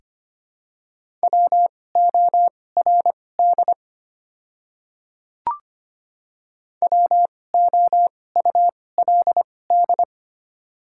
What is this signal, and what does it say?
Morse code, heard as a keyed tone, about 25 wpm, spaced as for 18 wpm, 700 Hz. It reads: WORD WOULD
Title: 2x Character Spacing - Top 100 Words - 25wpm